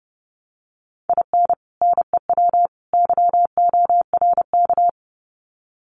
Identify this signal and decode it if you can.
Morse code, heard as a keyed tone, about 29 wpm, 700 Hz. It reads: IN NEW YORK
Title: Sets of 3 Words 30wpm